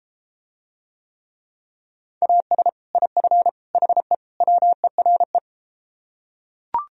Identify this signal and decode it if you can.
Morse code, heard as a keyed tone, about 34 wpm, 700 Hz. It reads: AS IF HE WERE